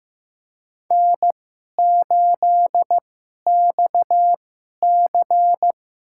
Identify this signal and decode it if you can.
Morse code, heard as a keyed tone, about 15 wpm, spaced as for 11 wpm, 700 Hz. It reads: N8XC
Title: USA Call Signs - 2x Character Spacing - 15wpm